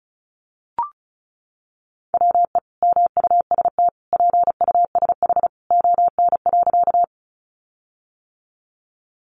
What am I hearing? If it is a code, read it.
Morse code, heard as a keyed tone, about 34 wpm, 700 Hz. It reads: WE MUST PUSH ON.
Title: Sentences from Top 1300 Words 35wpm